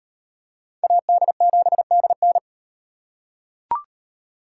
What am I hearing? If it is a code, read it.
Morse code, heard as a keyed tone, about 38 wpm, 700 Hz. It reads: AD7DN